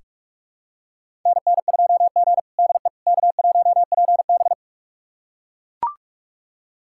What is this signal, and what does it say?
Morse code, heard as a keyed tone, about 41 wpm, 700 Hz. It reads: NN2G DE K1PB
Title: USA Callsign DE Callsign - 4 Characters - 45wpm